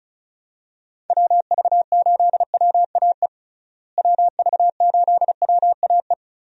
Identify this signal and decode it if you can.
Morse code, heard as a keyed tone, about 36 wpm, 700 Hz. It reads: WV8WAE WV8WAE
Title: USA Call Signs - 6 Characters - Repeated 2x - 35wpm